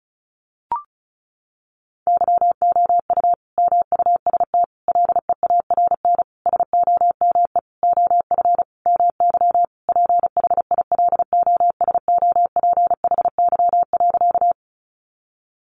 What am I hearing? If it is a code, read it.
Morse code, heard as a keyed tone, about 34 wpm, 700 Hz. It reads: YOU MUST LEARN SOME OF MY PHILOSOPHY.
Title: Sentences from Top 900 Words 35wpm